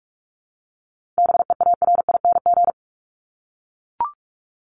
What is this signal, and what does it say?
Morse code, heard as a keyed tone, about 44 wpm, 700 Hz. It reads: BEARING
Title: ING Suffix - Top 100 Words with Base Words 45wpm